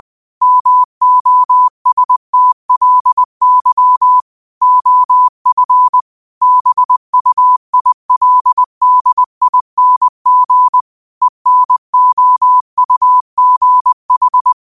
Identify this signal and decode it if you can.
Morse code, heard as a keyed tone, about 20 wpm, 1000 Hz. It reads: MOSTLY OF BUILDING ENOUGH